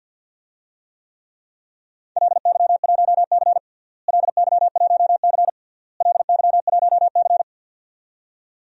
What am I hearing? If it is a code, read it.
Morse code, heard as a keyed tone, about 52 wpm, 700 Hz. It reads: LY1C LY1C LY1C
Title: DXCC-100 Call Signs - 4 Characters - Repeated 3x - 50wpm